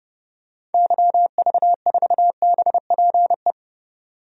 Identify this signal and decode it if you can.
Morse code, heard as a keyed tone, about 30 wpm, 700 Hz. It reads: YV4BPE
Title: DXCC-100 Call Signs - 6 Characters - 30wpm